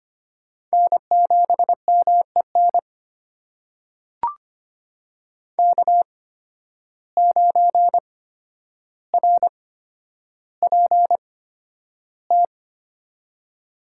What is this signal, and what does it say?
Morse code, heard as a keyed tone, about 25 wpm, spaced as for 11 wpm, 700 Hz. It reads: N7MEN K9RPT
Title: USA Call Signs - 5 Characters - 8x Character Spacing - 25wpm